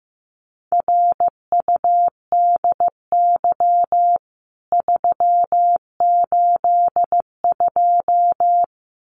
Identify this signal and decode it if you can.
Morse code, heard as a keyed tone, about 15 wpm, 700 Hz. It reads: RUDY 382
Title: CWT Exchange 15wpm